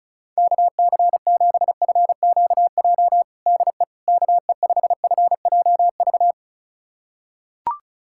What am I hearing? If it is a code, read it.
Morse code, heard as a keyed tone, about 35 wpm, 700 Hz. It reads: KC7FQJ DE KE5FJV
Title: USA Callsign DE Callsign - 6 Characters - 35wpm